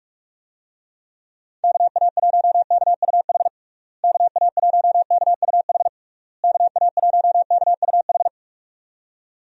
Morse code, heard as a keyed tone, about 41 wpm, 700 Hz. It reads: KA1KUH KA1KUH KA1KUH
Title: USA Call Signs - 6 Characters - Repeated 3x - 45wpm